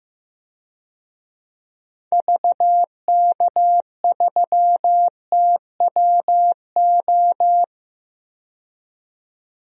Morse code, heard as a keyed tone, about 15 wpm, 700 Hz. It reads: VK3TWO
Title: DXCC-100 Call Signs - 6 Characters - 15wpm